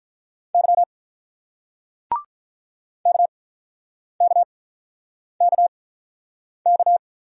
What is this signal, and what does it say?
Morse code, heard as a keyed tone, about 48 wpm, spaced as for 9 wpm, 690 Hz. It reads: Y KKKK